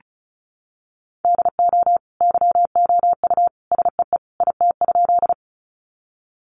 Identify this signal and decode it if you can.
Morse code, heard as a keyed tone, about 34 wpm, 700 Hz. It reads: DO YOU SEE IT?
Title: Sentences from Top 100 Words 35wpm